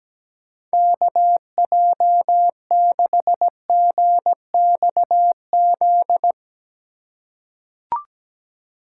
Morse code, heard as a keyed tone, about 17 wpm, 700 Hz. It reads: KJ6GXZ
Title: USA Call Signs - 6 Characters - 3x Character Spacing - 17wpm